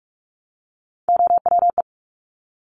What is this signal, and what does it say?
Morse code, heard as a keyed tone, about 46 wpm, 700 Hz. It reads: OWE
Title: Top Three Letter Combinations in Words 45wpm